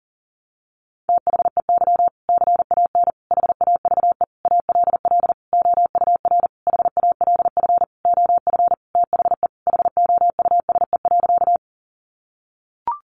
Morse code, heard as a keyed tone, about 40 wpm, 700 Hz. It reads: THEY CAN HAVE ALL OUR HALF OF THE HOUSE.
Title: Sentences from Top 400 Words 40wpm